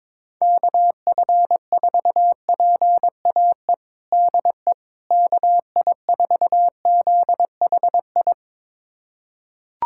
Morse code, heard as a keyed tone, about 21 wpm, 700 Hz. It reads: KF4PAE DE KI4ZHI
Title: USA Callsign DE Callsign - 6 Characters - 22wpm